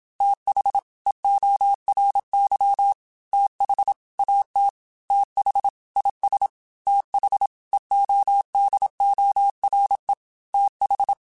Morse code, heard as a keyed tone, about 26 wpm, 800 Hz. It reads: TH EORY TH AT TH IS TH EODORE TH
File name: TH Runs sp / TH Runs sp.mp3s / TH Runs sp @25